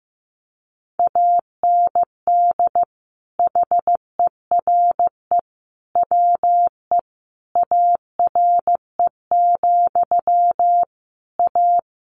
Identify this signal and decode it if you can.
Morse code, heard as a keyed tone, about 15 wpm, 700 Hz. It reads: AND HERE WE ARE, A
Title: A Sound Of Thunder 15wpm